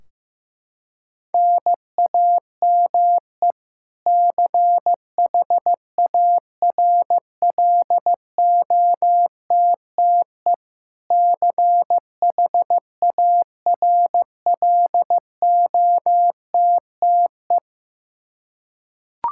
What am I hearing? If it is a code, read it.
Morse code, heard as a keyed tone, about 15 wpm, 700 Hz. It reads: NAME CHARLOTTE CHARLOTTE
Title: Practice QSO - Exchange 1 - Beginning - 15wpm